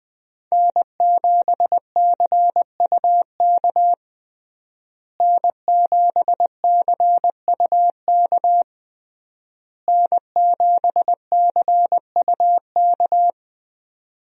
Morse code, heard as a keyed tone, about 21 wpm, 700 Hz. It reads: N7CUK N7CUK N7CUK
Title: USA Call Signs - 5 Characters - Repeated 3x - 20wpm